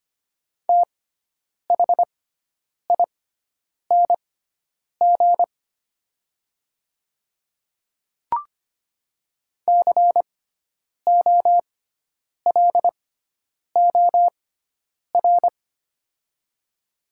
Morse code, heard as a keyed tone, about 25 wpm, spaced as for 9 wpm, 700 Hz. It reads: THING COLOR
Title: 6x Character Spacing - Top 300 Words - 25wpm